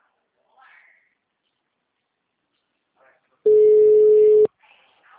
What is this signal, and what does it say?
Ringback tone (Europe, 425 Hz), with speech.
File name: out- -2052- - - .29743